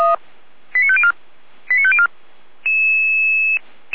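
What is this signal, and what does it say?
Five-tone selective call (ZVEI), heard as quick runs of five tones, tones of about 70 ms. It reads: 89504 89504
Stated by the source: ZVEI five-tone call 89001 received on SDR, logged by Feuerwehr Orlamünde.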